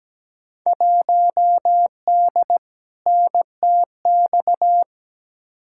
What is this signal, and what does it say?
Morse code, heard as a keyed tone, about 17 wpm, 700 Hz. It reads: 1D NTX
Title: Field Day Exchanges - 17wpm